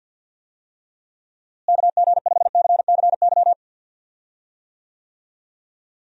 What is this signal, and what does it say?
Morse code, heard as a keyed tone, about 52 wpm, 700 Hz. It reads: KG5CCY
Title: USA Call Signs - 6 Characters - Repeated 2x - 50wpm